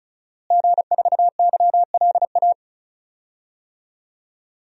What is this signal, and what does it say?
Morse code, heard as a keyed tone, about 36 wpm, 700 Hz. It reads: G4YLA
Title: DXCC-100 Call Signs - 5 Characters - Repeated 3x - 35wpm